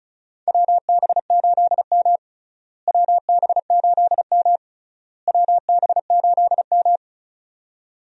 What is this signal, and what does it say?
Morse code, heard as a keyed tone, about 34 wpm, 700 Hz. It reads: WB8M WB8M WB8M
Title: USA Call Signs - 4 Characters - Repeated 3x - 35wpm